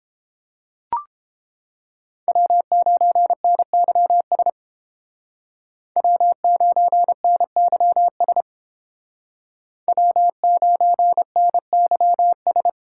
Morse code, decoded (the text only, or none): W9NYS W9NYS W9NYS